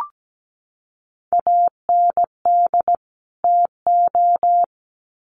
Morse code, heard as a keyed tone, about 17 wpm, 700 Hz. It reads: AND TO
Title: Sets of 2 Words 17wpm